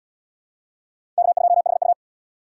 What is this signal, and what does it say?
Morse code, heard as a keyed tone, about 66 wpm, 700 Hz. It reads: K2HU